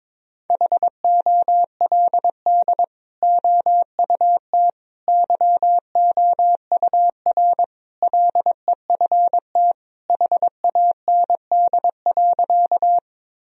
Morse code, heard as a keyed tone, about 22 wpm, 700 Hz. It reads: HOLD OUT YOUR LEFT HAND.
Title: Sentences from Top 1000 Words - Review Part 2 - 22wpm